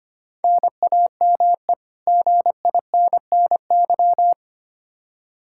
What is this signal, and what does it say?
Morse code, heard as a keyed tone, about 25 wpm, 700 Hz. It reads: NAME GINNY
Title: Practice QSO - Exchange 1 - Beginning - 25wpm